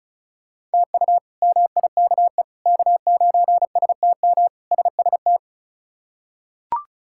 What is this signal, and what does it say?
Morse code, heard as a keyed tone, about 34 wpm, 700 Hz. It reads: TU MIKE K9STM SST
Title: K1USN SST QSO Practice - 35wpm